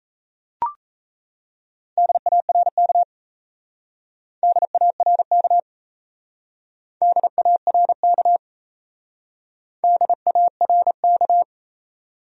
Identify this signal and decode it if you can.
Morse code, heard as a keyed tone, about 34 wpm, 700 Hz. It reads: DARK DARK DARK DARK
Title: Speed-Racing Top 500 Words 28wpm